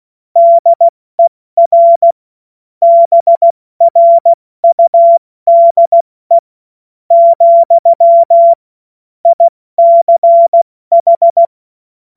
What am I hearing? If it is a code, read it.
Morse code, heard as a keyed tone, about 16 wpm, spaced as for 13 wpm, 680 Hz. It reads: DER BRUDE, ICH
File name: Morsekurs / Witze / Witz 06. Juli 2021, 70 F80